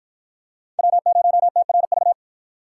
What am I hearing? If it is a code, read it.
Morse code, heard as a keyed tone, about 53 wpm, 700 Hz. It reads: W0TRV